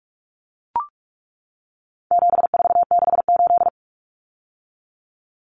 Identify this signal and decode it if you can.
Morse code, heard as a keyed tone, about 44 wpm, 700 Hz. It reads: 7468
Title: Numbers - 4-Digit 45wpm